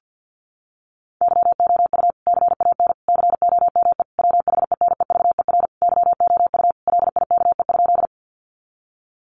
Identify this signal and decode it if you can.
Morse code, heard as a keyed tone, about 50 wpm, 700 Hz. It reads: YOU CAN COME WHENEVER YOU LIKE?
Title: Sentences from Top 1200 Words 50wpm